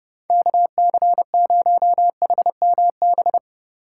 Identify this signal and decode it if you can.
Morse code, heard as a keyed tone, about 29 wpm, 700 Hz. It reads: KC0HMB